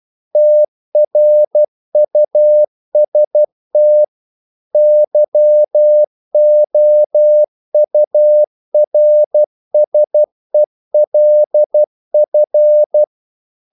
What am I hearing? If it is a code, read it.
Morse code, heard as a keyed tone, about 12 wpm, 600 Hz. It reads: TRUST YOURSELF
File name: CW / Practice / If12